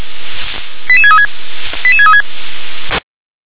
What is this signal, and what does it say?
Five-tone selective call (ZVEI), heard as quick runs of five tones, tones of about 70 ms. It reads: 88636 88636
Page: ZVEI